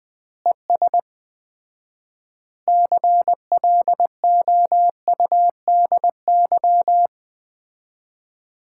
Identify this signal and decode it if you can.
Morse code, heard as a keyed tone, about 20 wpm, 700 Hz. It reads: ES CLOUDY
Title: Common QSO Phrases - 4x Word-Spacing - 20wpm